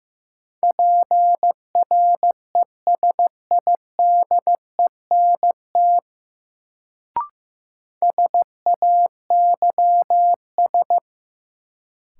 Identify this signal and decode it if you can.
Morse code, heard as a keyed tone, about 15 wpm, 700 Hz. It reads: PRESIDENT SAYS
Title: Sets of 2 Words - New Word Drill - 15wpm